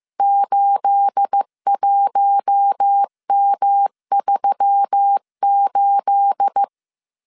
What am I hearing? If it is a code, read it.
Morse code, heard as a keyed tone, about 15 wpm, 800 Hz. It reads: 81M38
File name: CW / DET / 1 9 1500C S A 14wpm 3222